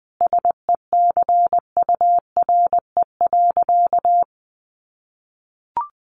Morse code, heard as a keyed tone, about 20 wpm, 700 Hz. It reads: SECURE.
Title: Sentences from Top 1200 Words 20wpm